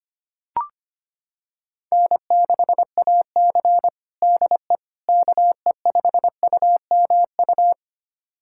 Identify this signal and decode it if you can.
Morse code, heard as a keyed tone, about 25 wpm, 700 Hz. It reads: N6AC DE KE5UMU
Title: USA Callsign DE Callsign 25wpm